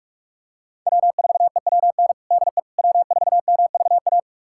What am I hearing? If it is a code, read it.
Morse code, heard as a keyed tone, about 43 wpm, 700 Hz. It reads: W4EWN DE W4MVA
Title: USA Callsign DE Callsign - 5 Characters - 45wpm